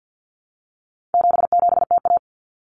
Morse code, heard as a keyed tone, about 50 wpm, 700 Hz. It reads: 77TA
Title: Common Cut Numbers - 4-Digit 50wpm